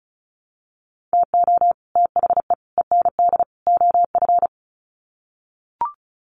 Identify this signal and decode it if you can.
Morse code, heard as a keyed tone, about 34 wpm, 700 Hz. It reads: TO THE END OF
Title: Sets of 4 Words 35wpm